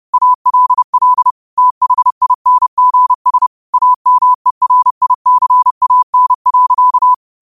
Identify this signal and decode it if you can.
Morse code, heard as a keyed tone, about 30 wpm, 1000 Hz. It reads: ALL THINGS AMERICAN.